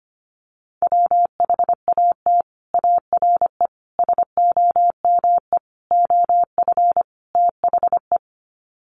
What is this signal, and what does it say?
Morse code, heard as a keyed tone, about 25 wpm, 700 Hz. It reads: WHAT ARE SOME OF THE
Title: Sets of 5 Words Encore 25wpm